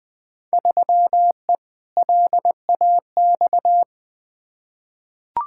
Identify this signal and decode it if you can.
Morse code, heard as a keyed tone, about 21 wpm, 700 Hz. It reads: 3E LAX
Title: Field Day Exchanges Encore - 20wpm